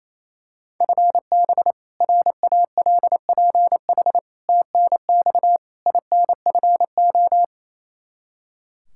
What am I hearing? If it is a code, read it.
Morse code, heard as a keyed tone, about 28 wpm, 700 Hz. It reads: FB RALPH TNX INFO
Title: Practice QSO - Exchange 3 - More Info - 28wpm